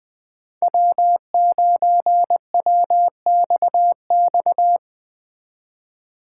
Morse code, heard as a keyed tone, about 20 wpm, 700 Hz. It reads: W9WXX